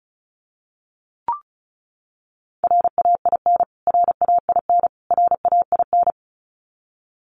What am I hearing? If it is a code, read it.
Morse code, heard as a keyed tone, about 34 wpm, 700 Hz. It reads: RAIN RAIN RAIN
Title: Top 300 Words Repeated 35wpm